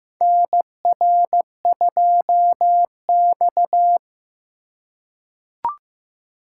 Morse code, heard as a keyed tone, about 15 wpm, 700 Hz. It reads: NR2X
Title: USA Call Signs - 4 Characters - 3x Character Spacing - 15wpm